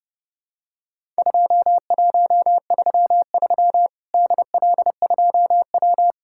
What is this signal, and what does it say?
Morse code, heard as a keyed tone, about 32 wpm, 700 Hz. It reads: 2133 DL2W
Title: Worked All Europe DX Contest - QTC Exchange - 30wpm